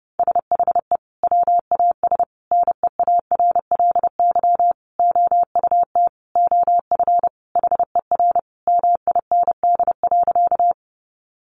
Morse code, heard as a keyed tone, about 30 wpm, 700 Hz. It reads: SHE WAS NEARLY OUT OF HER MIND.
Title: Sentences from Top 1300 Words 30wpm